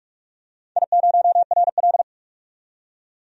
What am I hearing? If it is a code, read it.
Morse code, heard as a keyed tone, about 41 wpm, 700 Hz. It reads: I0RL
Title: DXCC-100 Call Signs - 4 Characters - 45wpm